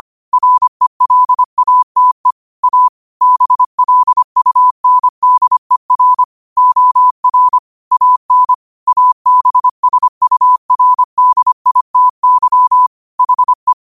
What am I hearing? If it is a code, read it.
Morse code, heard as a keyed tone, about 25 wpm, 1000 Hz. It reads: RELATE A BLUNDER OR AN ABSURDITY HE
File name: qotdfiles / 25wpm